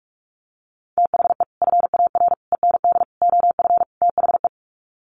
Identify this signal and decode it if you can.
Morse code, heard as a keyed tone, about 44 wpm, 700 Hz. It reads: THE FAR END OF THE
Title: Sets of 5 Words 45wpm